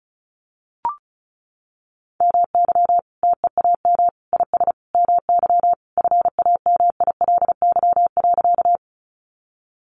Morse code, heard as a keyed tone, about 34 wpm, 700 Hz. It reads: MY TEAM IS MY FAMILY.
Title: Sentences from Top 1500 Words 35wpm